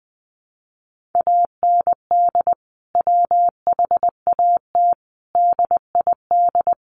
Morse code, heard as a keyed tone, about 20 wpm, 700 Hz. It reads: AND WHAT DID
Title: Sentences from Top 1500 Words 20wpm